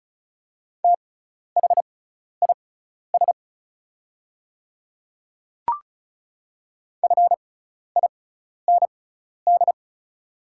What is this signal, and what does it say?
Morse code, heard as a keyed tone, about 36 wpm, spaced as for 12 wpm, 700 Hz. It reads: THIS FIND
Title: 6x Character Spacing - Top 100 Words - 35wpm